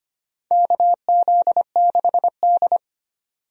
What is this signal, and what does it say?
Morse code, heard as a keyed tone, about 25 wpm, 700 Hz. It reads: KZ6D
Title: USA Call Signs - 4 Characters - 25wpm